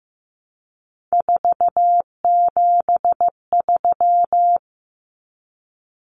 Morse code, heard as a keyed tone, about 15 wpm, 700 Hz. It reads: 473
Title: Numbers - 3-Digit 15wpm